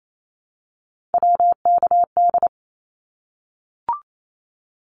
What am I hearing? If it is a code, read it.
Morse code, heard as a keyed tone, about 28 wpm, 700 Hz. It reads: WKD
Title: Abbreviations Repeated 3x - 28wpm